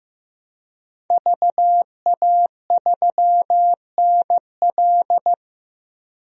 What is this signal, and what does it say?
Morse code, heard as a keyed tone, about 16 wpm, 700 Hz. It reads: VA3NL
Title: DXCC-100 Call Signs - 5 Characters - Repeated 3x - 15wpm